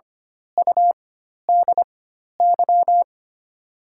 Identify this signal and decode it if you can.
Morse code, heard as a keyed tone, about 25 wpm, spaced as for 12 wpm, 700 Hz. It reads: UDY
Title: 4x Character Spacing - Top 300 Words - 25wpm